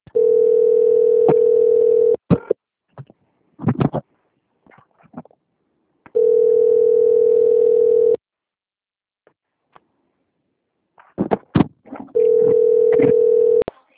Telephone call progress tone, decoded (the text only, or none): ringback tone